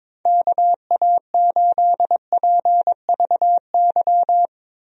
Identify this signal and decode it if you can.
Morse code, heard as a keyed tone, about 22 wpm, 700 Hz. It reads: KA8PVY
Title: USA Call Signs - 6 Characters - Repeated 3x - 22wpm